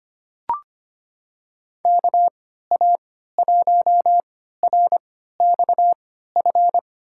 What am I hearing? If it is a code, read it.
Morse code, heard as a keyed tone, about 25 wpm, spaced as for 14 wpm, 700 Hz. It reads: KA1RXF
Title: USA Call Signs - 6 Characters - 3x Character Spacing - 25wpm